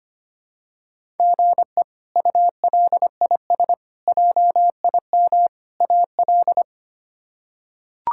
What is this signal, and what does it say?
Morse code, heard as a keyed tone, about 25 wpm, 700 Hz. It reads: GE ULIS JIM AL
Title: K1USN SST QSO Practice - 25wpm